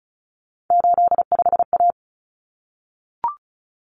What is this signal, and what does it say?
Morse code, heard as a keyed tone, about 34 wpm, 700 Hz. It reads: 85A
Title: Common Cut Numbers - 3-Digit 35wpm